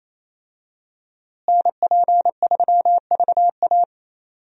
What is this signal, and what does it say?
Morse code, heard as a keyed tone, about 28 wpm, 700 Hz. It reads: NP3VA
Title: USA Call Signs - 5 Characters - Repeated 2x - 28wpm